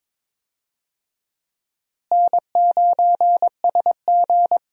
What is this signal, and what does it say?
Morse code, heard as a keyed tone, about 22 wpm, 700 Hz. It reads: N9SG
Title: USA Call Signs - 4 Characters - Repeated 2x - 22wpm